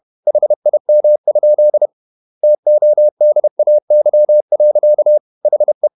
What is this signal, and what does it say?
Morse code, heard as a keyed tone, about 31 wpm, 600 Hz. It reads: HIM? TODAY. HE
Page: Never 31wpm | CW med Gnister